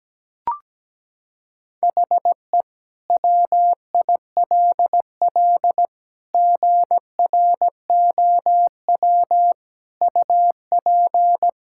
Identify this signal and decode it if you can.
Morse code, heard as a keyed tone, about 17 wpm, 700 Hz. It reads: HE WILL GROW UP
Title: Sentences from Top 1000 Words - Review Part 6 - 17wpm